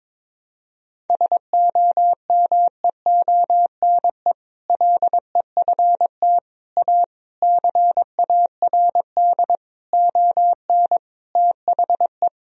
Morse code, heard as a keyed tone, about 22 wpm, 700 Hz. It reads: SOMEONE LEFT A CARD ON THE
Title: Sentences from Top 1000 Words - Review Part 4 - 22wpm